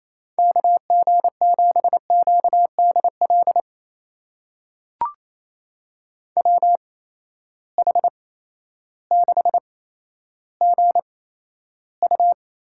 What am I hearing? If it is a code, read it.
Morse code, heard as a keyed tone, about 28 wpm, spaced as for 13 wpm, 700 Hz. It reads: KG7QDL WH6GU
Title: USA Call Signs - 8x Character Spacing - 28wpm